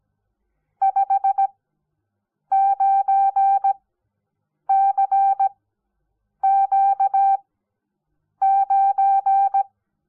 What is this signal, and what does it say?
Morse code, heard as a keyed tone, about 17 wpm, spaced as for 7 wpm, 780 Hz. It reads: 59CQ9
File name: Onlinelehrgang / morsen / DJ4UF Morsen05